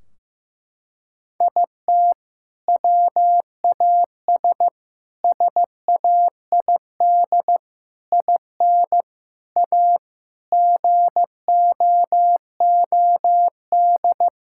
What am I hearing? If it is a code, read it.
Morse code, heard as a keyed tone, about 15 wpm, 700 Hz. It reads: IT WAS SAID IN A GOOD
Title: Sentences from Top 1600 Words 15wpm